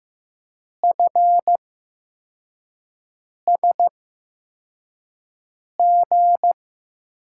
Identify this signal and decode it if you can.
Morse code, heard as a keyed tone, about 15 wpm, spaced as for 4 wpm, 700 Hz. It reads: FSG
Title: USA Call Signs - 8x Character Spacing - 15wpm